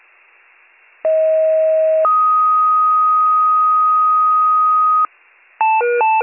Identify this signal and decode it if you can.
Two-tone page, heard as a one-second tone, then a three-second tone, about 640 then 1250 Hz. It is heard from 1.0 to 5.1 s.